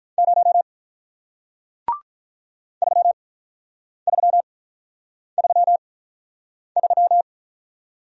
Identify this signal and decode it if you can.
Morse code, heard as a keyed tone, about 43 wpm, 700 Hz. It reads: 0 3 3 3 3
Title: Speed-Racing Single Letter-Number 35wpm